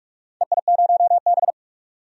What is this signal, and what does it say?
Morse code, heard as a keyed tone, about 45 wpm, 700 Hz. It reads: EI0B
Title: DXCC-100 Call Signs - 4 Characters - 45wpm